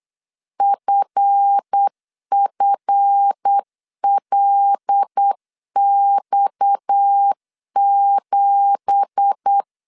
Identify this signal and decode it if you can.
Morse code, heard as a keyed tone, about 8 wpm, 800 Hz. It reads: FFLX7
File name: CW / DET / 1 9 1500C S A 8wpm 5637